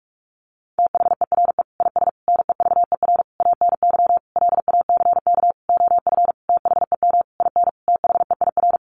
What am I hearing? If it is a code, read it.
Morse code, heard as a keyed tone, about 44 wpm, 700 Hz. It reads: THERE IS NEVER ANY LACK OF THEM IN THEIR
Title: Sentences from Top 1100 Words 45wpm